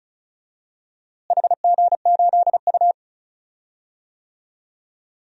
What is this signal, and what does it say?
Morse code, heard as a keyed tone, about 34 wpm, 700 Hz. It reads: HG8U
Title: DXCC-100 Call Signs - 4 Characters - Repeated 3x - 35wpm